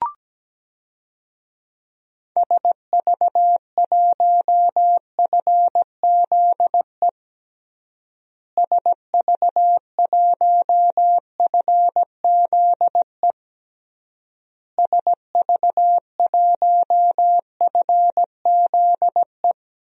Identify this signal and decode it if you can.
Morse code, heard as a keyed tone, about 17 wpm, 700 Hz. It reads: SV1FZE SV1FZE SV1FZE
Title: DXCC-100 Call Signs - 6 Characters - Repeated 3x - 17wpm